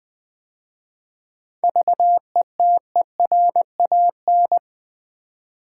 Morse code, heard as a keyed tone, about 20 wpm, 700 Hz. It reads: VETERAN
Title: 7 Letter Words - First 3 Letters Given - 20wpm